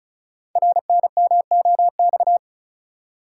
Morse code, heard as a keyed tone, about 34 wpm, 700 Hz. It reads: RNMOX
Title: Code Groups - 5 Letters - 35wpm